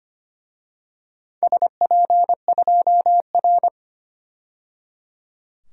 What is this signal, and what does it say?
Morse code, heard as a keyed tone, about 25 wpm, 700 Hz. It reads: SP2R
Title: DXCC-100 Call Signs - 4 Characters - 25wpm